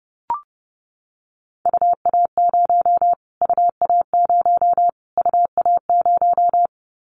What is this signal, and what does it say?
Morse code, heard as a keyed tone, about 30 wpm, 700 Hz. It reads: UA0 UA0 UA0
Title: Callsign Prefixes for Top 25 DXCC - Repeated 3x - 30wpm